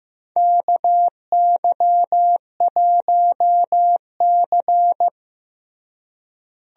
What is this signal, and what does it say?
Morse code, heard as a keyed tone, about 15 wpm, 700 Hz. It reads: KY1C
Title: USA Call Signs - 4 Characters - Repeated 2x - 15wpm